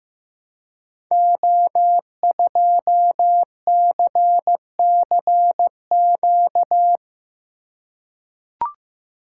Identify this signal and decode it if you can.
Morse code, heard as a keyed tone, about 15 wpm, 700 Hz. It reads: O2CCQ
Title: Code Groups - 5 Letter:Numbers - 15wpm